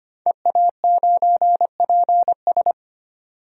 Morse code, heard as a keyed tone, about 25 wpm, 700 Hz. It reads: EA9PS
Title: DXCC-100 Call Signs - 5 Characters - 25wpm